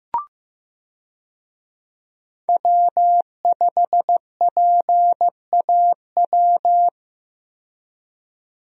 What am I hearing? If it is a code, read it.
Morse code, heard as a keyed tone, about 15 wpm, 700 Hz. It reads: W5PAW
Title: USA Call Signs - 5 Characters - 15wpm